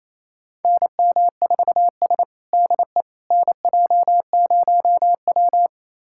Morse code, heard as a keyed tone, about 28 wpm, 700 Hz. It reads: NM4S DE NJ0W
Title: USA Callsign DE Callsign - 4 Characters - 28wpm